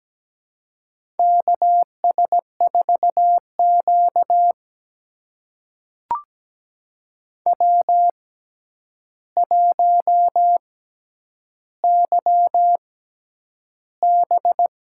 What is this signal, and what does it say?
Morse code, heard as a keyed tone, about 17 wpm, spaced as for 9 wpm, 700 Hz. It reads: KS4Q W1YB
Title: USA Call Signs - 4 Characters - 6x Character Spacing - 17wpm